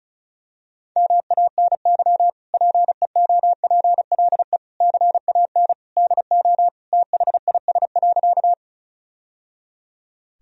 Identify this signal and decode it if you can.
Morse code, heard as a keyed tone, about 36 wpm, 700 Hz. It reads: MANY PEOPLE CAN DO THIS.
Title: Sentences from Top 1000 Words - Review Part 1 - 35wpm